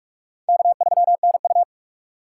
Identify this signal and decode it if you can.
Morse code, heard as a keyed tone, about 47 wpm, 700 Hz. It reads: K3NU
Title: USA Call Signs - 4 Characters - 6x Character Spacing - 45wpm